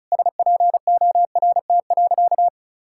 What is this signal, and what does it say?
Morse code, heard as a keyed tone, about 34 wpm, 700 Hz. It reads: SPORT.
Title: Sentences from Top 1600 Words 35wpm